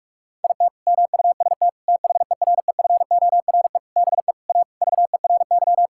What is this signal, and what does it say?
Morse code, heard as a keyed tone, about 41 wpm, 700 Hz. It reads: IT MUST THEREFORE BE A VERY
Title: Sentences from Top 1000 Words - Review Part 6 - 45wpm